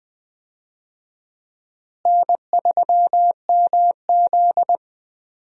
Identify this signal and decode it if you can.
Morse code, heard as a keyed tone, about 21 wpm, 700 Hz. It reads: N3MZ
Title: USA Call Signs - 4 Characters - Repeated 2x - 20wpm